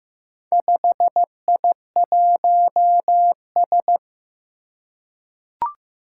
Morse code, heard as a keyed tone, about 15 wpm, 700 Hz. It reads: 5I1S
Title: Four Challenging Letter-Number Combinations 15wpm